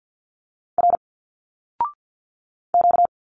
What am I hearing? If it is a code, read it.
Morse code, heard as a keyed tone, about 51 wpm, 700 Hz. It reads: R Q
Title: Single Letters 50wpm